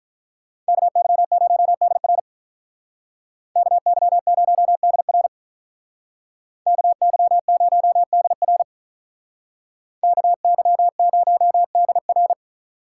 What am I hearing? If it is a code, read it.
Morse code, heard as a keyed tone, about 43 wpm, 700 Hz. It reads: KY0DR KY0DR KY0DR KY0DR
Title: Speed Racing - USA Call Signs 35wpm